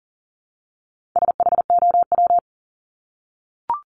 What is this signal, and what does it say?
Morse code, heard as a keyed tone, about 40 wpm, 700 Hz. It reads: SHOW
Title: Top 200 Words 40wpm